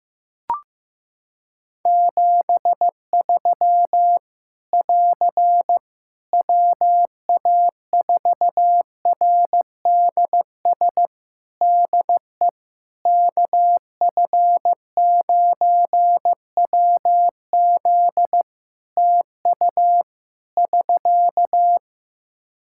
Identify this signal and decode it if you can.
Morse code, heard as a keyed tone, about 15 wpm, 700 Hz. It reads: 73 <AR> WA4RDS DE KF9WZ TU <SK>
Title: K1USN SST QSO Practice 15wpm